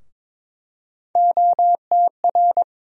Morse code, heard as a keyed tone, about 21 wpm, 700 Hz. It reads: OTR
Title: USA Call Sign Postfixes - 3 Letters 22wpm